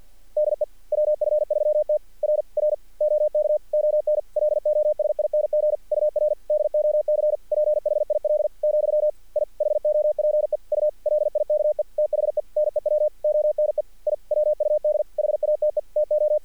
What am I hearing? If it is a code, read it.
Morse code, heard as a keyed tone, about 46 wpm, 600 Hz. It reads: DE WF3T R R OK ON LOSING UR DOC PHIL , I HOPE U LIKE THE NEW ONE I WUD HATE TO
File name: archived / CRSnet